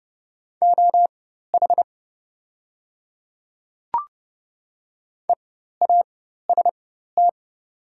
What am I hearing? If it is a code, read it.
Morse code, heard as a keyed tone, about 29 wpm, spaced as for 14 wpm, 700 Hz. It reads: OH EAST
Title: 4x Character Spacing - Top 500 Words - 30wpm